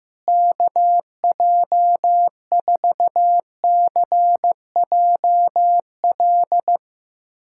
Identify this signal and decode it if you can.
Morse code, heard as a keyed tone, about 15 wpm, 700 Hz. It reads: KJ4CJL
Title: USA Callsign DE Callsign - 6 Characters - 15wpm